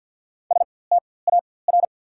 Morse code, heard as a keyed tone, about 48 wpm, spaced as for 23 wpm, 700 Hz. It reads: STAR
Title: 4x Character Spacing - Top 500 Words - 50wpm